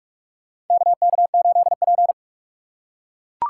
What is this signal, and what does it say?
Morse code, heard as a keyed tone, about 47 wpm, 700 Hz. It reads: KK8P
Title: USA Call Signs - 6x Character Spacing - 45wpm